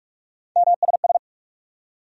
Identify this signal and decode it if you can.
Morse code, heard as a keyed tone, about 44 wpm, 700 Hz. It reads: MSS
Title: USA Call Sign Postfixes - 3 Letters 45wpm